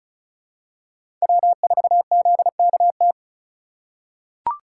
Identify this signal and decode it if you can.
Morse code, heard as a keyed tone, about 36 wpm, 700 Hz. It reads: W4ZKT
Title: USA Call Signs - 5 Characters - 8x Character Spacing - 35wpm